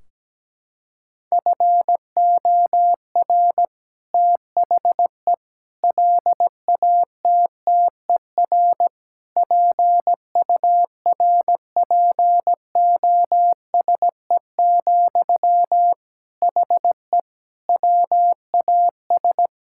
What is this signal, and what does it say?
Morse code, heard as a keyed tone, about 17 wpm, 700 Hz. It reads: FOR THE LATTER PURPOSE, HE WAS
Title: Sentences from Top 1600 Words 17wpm